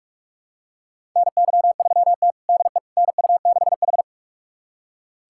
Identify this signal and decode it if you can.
Morse code, heard as a keyed tone, about 47 wpm, 700 Hz. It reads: NY3T DE NU6H
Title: USA Callsign DE Callsign - 4 Characters - 45wpm